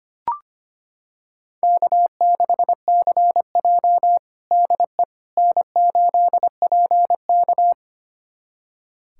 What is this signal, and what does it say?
Morse code, heard as a keyed tone, about 25 wpm, 700 Hz. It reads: K6CJ DE N8PK
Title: USA Callsign DE Callsign - 4 Characters - 25wpm